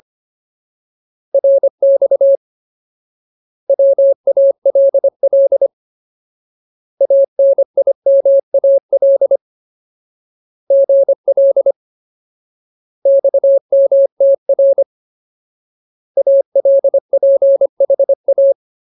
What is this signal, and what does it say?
Morse code, heard as a keyed tone, about 25 wpm, 550 Hz. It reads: RX WALL ANIMAL GL XMTR ALPHA
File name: Ultimate Word List / 25WPM Ultimate